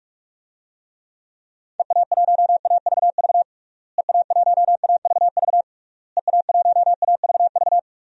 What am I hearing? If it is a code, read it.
Morse code, heard as a keyed tone, about 46 wpm, 700 Hz. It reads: EA1AVV EA1AVV EA1AVV
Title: DXCC-100 Call Signs - 6 Characters - Repeated 3x - 45wpm